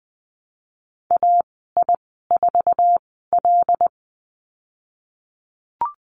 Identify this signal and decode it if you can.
Morse code, heard as a keyed tone, about 20 wpm, spaced as for 14 wpm, 700 Hz. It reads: AI4L
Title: 231 Lesson 37g - Letter x - Call Signs 20wpm